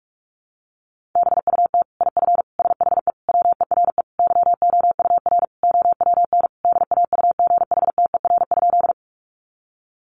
Morse code, heard as a keyed tone, about 44 wpm, 700 Hz. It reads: BUT IF SHE WERE YOUR OWN DAUGHTER?
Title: Sentences from Top 1500 Words 45wpm